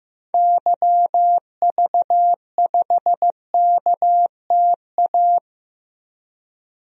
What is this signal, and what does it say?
Morse code, heard as a keyed tone, about 15 wpm, 700 Hz. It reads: YV5KTA
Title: DXCC-100 Call Signs - 6 Characters - 15wpm